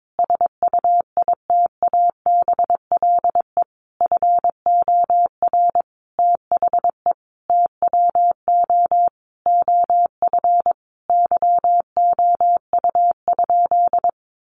Morse code, heard as a keyed tone, about 22 wpm, 700 Hz. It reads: SUITABLE FOR THE TWO OF YOU?
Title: Sentences from Top 1400 Words 22wpm